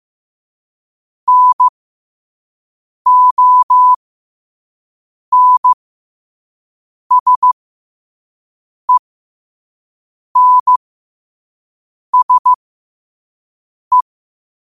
Morse code, heard as a keyed tone, about 15 wpm, spaced as for 5 wpm, 1000 Hz. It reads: NONSENSE
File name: qotdfiles / 5wpm